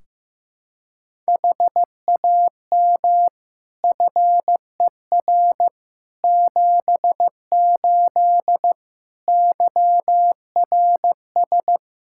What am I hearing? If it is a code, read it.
Morse code, heard as a keyed tone, about 16 wpm, 700 Hz. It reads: HAM FER 78 YRS
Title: Practice QSO - Exchange 3 - More Info - 15wpm